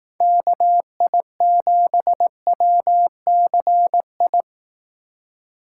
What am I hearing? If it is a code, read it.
Morse code, heard as a keyed tone, about 18 wpm, 700 Hz. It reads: KI7WCI
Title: Speed Racing - USA 2x3 Call Signs 15wpm